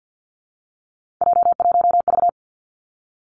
Morse code, heard as a keyed tone, about 50 wpm, 700 Hz. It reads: J1V
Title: Three Challenging Letter-Number Combinations 50wpm